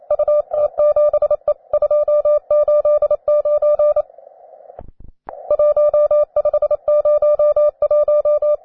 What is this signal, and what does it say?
Morse code, heard as a keyed tone, about 28 wpm, 600 Hz. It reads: UT7E 289 1501
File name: wae / cw beginner / jh3aiu cw